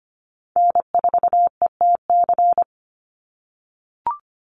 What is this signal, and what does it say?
Morse code, heard as a keyed tone, about 25 wpm, 700 Hz. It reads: N4ETC
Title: Call Signs - 25wpm